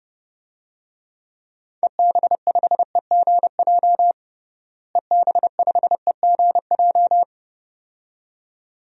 Morse code, heard as a keyed tone, about 29 wpm, 700 Hz. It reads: EB5EGJ EB5EGJ
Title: DXCC-100 Call Signs - 6 Characters - Repeated 2x - 30wpm